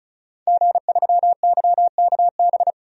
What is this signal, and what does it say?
Morse code, heard as a keyed tone, about 36 wpm, 700 Hz. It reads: G3YKB
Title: DXCC-100 Call Signs - 5 Characters - 35wpm